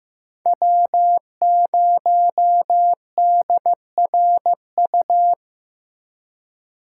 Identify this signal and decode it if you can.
Morse code, heard as a keyed tone, about 15 wpm, 700 Hz. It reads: W0DRU
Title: USA Call Signs - 8x Character Spacing - 15wpm